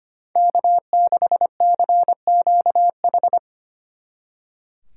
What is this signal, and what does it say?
Morse code, heard as a keyed tone, about 28 wpm, 700 Hz. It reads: K6CQH
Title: Speed Racing - USA 1x3 Call Signs 25wpm